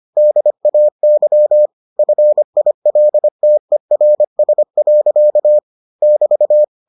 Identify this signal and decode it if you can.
Morse code, heard as a keyed tone, about 25 wpm, 600 Hz. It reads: DAY FILTERS. <BT>